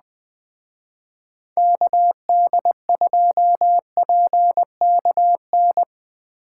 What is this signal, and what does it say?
Morse code, heard as a keyed tone, about 20 wpm, 700 Hz. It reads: KD2PKN
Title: Speed Racing - USA 2x3 Call Signs 17wpm